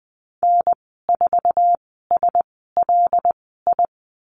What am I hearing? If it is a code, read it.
Morse code, heard as a keyed tone, about 20 wpm, spaced as for 15 wpm, 700 Hz. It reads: N4SLI
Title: 154 Lesson 25g - Letter p - Call Signs 20wpm